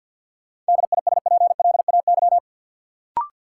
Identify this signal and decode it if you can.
Morse code, heard as a keyed tone, about 50 wpm, 700 Hz. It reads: DISPLAY
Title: Sentences from Top 1200 Words - New Word Drill Alt - 50wpm